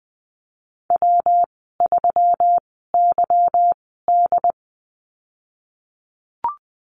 Morse code, heard as a keyed tone, about 20 wpm, spaced as for 15 wpm, 700 Hz. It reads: W3YD
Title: 167 Lesson 27g - Number 7 - Call Signs 20wpm